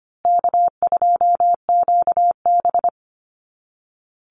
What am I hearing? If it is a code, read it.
Morse code, heard as a keyed tone, about 25 wpm, 700 Hz. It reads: K2QB
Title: Easy Call Signs Repeated 3x - 25wpm